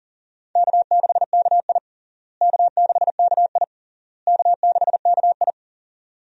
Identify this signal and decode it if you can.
Morse code, heard as a keyed tone, about 40 wpm, 700 Hz. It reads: K6KI K6KI K6KI
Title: USA Call Signs - 4 Characters - Repeated 3x - 40wpm